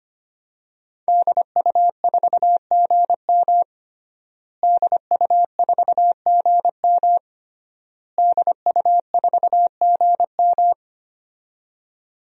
Morse code, heard as a keyed tone, about 25 wpm, 700 Hz. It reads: DU4GM DU4GM DU4GM
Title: DXCC-100 Call Signs - 5 Characters - Repeated 3x - 25wpm